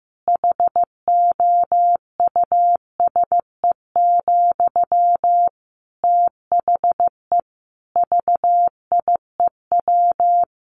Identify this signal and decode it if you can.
Morse code, heard as a keyed tone, about 15 wpm, 700 Hz. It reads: HOUSE, THE VIEW
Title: Sentences from Top 1100 Words 15wpm